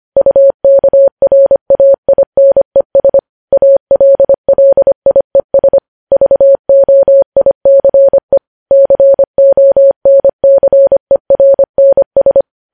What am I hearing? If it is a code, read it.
Morse code, heard as a keyed tone, about 25 wpm, 550 Hz. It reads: UKRAINES ALLIES VOICE CONCERNS
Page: News Headlines in Morse Code at 25 WPM